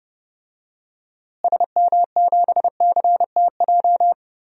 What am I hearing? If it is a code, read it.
Morse code, heard as a keyed tone, about 30 wpm, 700 Hz. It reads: SM7CTJ
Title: DXCC-100 Call Signs - 6 Characters - Repeated 2x - 30wpm